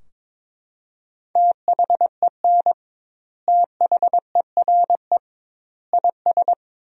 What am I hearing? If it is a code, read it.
Morse code, heard as a keyed tone, about 21 wpm, 700 Hz. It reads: THEN THERE IS
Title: 2x Word-Spacing - Sentences from Top 100 Words - 22wpm